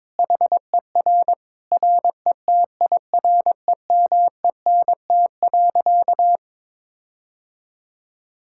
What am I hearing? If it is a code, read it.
Morse code, heard as a keyed tone, about 22 wpm, 700 Hz. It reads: HER RETIREMENT.
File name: Difficult Sentences with Hints - 22wpm